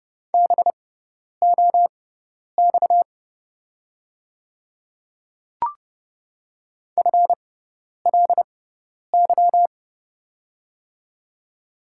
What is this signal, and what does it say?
Morse code, heard as a keyed tone, about 30 wpm, spaced as for 10 wpm, 690 Hz. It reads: BOX FLY
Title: 6x Character Spacing - Top 500 Words - 30wpm